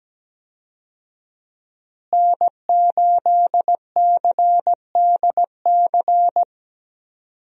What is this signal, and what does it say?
Morse code, heard as a keyed tone, about 17 wpm, 700 Hz. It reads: N8CDC
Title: USA Call Signs - 5 Characters - Repeated 2x - 17wpm